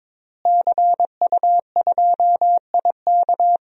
Morse code, heard as a keyed tone, about 22 wpm, 700 Hz. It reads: CU2IK
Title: DXCC-100 Call Signs - 5 Characters - Repeated 2x - 22wpm